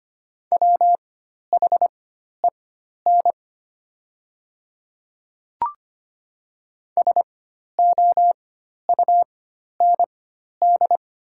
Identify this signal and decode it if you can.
Morse code, heard as a keyed tone, about 25 wpm, spaced as for 12 wpm, 700 Hz. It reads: WHEN SOUND
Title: 4x Character Spacing - Top 100 Words - 25wpm